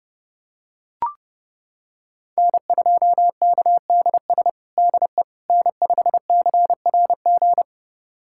Morse code, heard as a keyed tone, about 32 wpm, 700 Hz. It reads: N2KDS DE N5CRG
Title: USA Callsign DE Callsign - 5 Characters - 30wpm